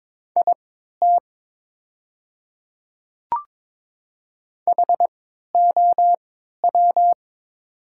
Morse code, heard as a keyed tone, about 22 wpm, spaced as for 12 wpm, 700 Hz. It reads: IT HOW
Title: 3x Character Spacing - Top 100 Words - 22wpm